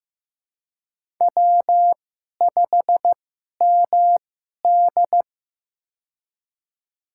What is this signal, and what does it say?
Morse code, heard as a keyed tone, about 15 wpm, spaced as for 11 wpm, 700 Hz. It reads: W5MD
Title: USA Call Signs - 4 Characters - 2x Character Spacing - 15wpm